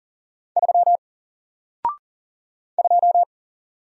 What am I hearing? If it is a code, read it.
Morse code, heard as a keyed tone, about 40 wpm, 700 Hz. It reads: 3 2
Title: Single Letter-Number Alt 40wpm